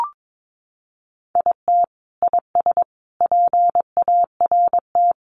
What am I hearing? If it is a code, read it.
Morse code, heard as a keyed tone, about 22 wpm, 700 Hz. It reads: IT IS PART
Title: Sentences from Top 1500 Words 22wpm